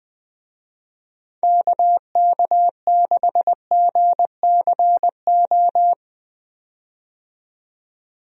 Morse code, heard as a keyed tone, about 20 wpm, 700 Hz. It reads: KK6GCO
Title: USA Call Signs - 6 Characters - 20wpm